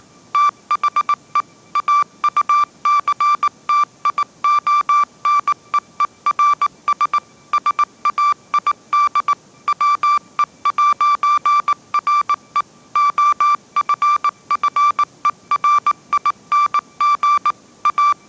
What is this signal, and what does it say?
Morse code, heard as a keyed tone, about 19 wpm, 1170 Hz. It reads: THE AUCTIONEERS SAID WE'RE OFFERING A